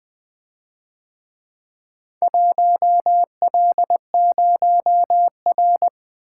Morse code, heard as a keyed tone, about 24 wpm, 700 Hz. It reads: 1L0R
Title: Speed-Racing - Four Letter-Number Combinations - 20wpm